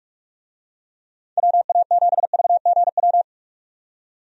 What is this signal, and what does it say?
Morse code, heard as a keyed tone, about 42 wpm, 700 Hz. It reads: WA7VGW
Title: USA Call Signs - 6 Characters - 45wpm